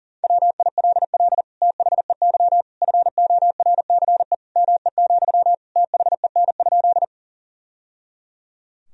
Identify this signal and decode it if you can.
Morse code, heard as a keyed tone, about 40 wpm, 700 Hz. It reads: WILL THEY FORCE ME, THEN?
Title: Sentences from Top 1000 Words - Review Part 3 - 40wpm